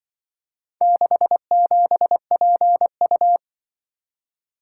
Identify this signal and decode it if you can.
Morse code, heard as a keyed tone, about 24 wpm, 700 Hz. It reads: 67PU
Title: Speed-Racing - Four Letter-Number Combinations - 20wpm